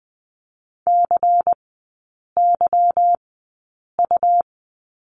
Morse code, heard as a keyed tone, about 20 wpm, spaced as for 8 wpm, 700 Hz. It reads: CYU
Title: 205 Lesson 33e - Letter j - Three Characters 20wpm